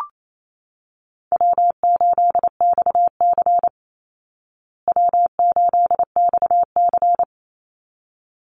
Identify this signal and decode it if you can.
Morse code, heard as a keyed tone, about 28 wpm, 700 Hz. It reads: W8XC W8XC
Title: Easy Call Signs Repeated 2x - 28wpm